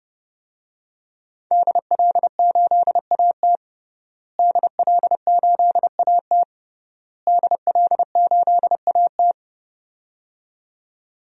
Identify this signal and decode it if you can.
Morse code, heard as a keyed tone, about 32 wpm, 700 Hz. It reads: DL8AT DL8AT DL8AT
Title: DXCC-100 Call Signs - 5 Characters - Repeated 3x - 30wpm